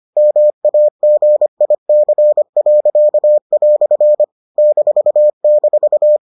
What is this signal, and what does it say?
Morse code, heard as a keyed tone, about 25 wpm, 600 Hz. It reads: MAGIC." --